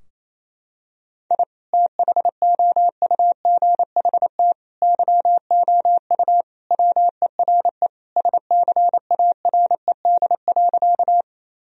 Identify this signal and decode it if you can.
Morse code, heard as a keyed tone, about 28 wpm, 700 Hz. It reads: I THOUGHT YOU WERE SCARED.
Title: Sentences from Top 1000 Words - Review Part 4 - 28wpm